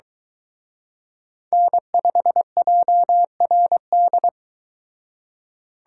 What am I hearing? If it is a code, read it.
Morse code, heard as a keyed tone, about 22 wpm, 700 Hz. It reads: N5JRD